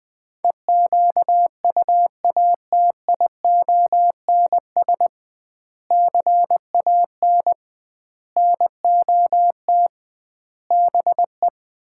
Morse code, heard as a keyed tone, about 20 wpm, 700 Hz. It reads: EQUATIONS CAN NOT BE
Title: 2x Word-Spacing - Sets of 5 Words - 20wpm